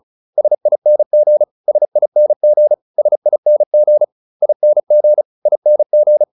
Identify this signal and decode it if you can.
Morse code, heard as a keyed tone, about 34 wpm, 600 Hz. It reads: SING SING SING ING ING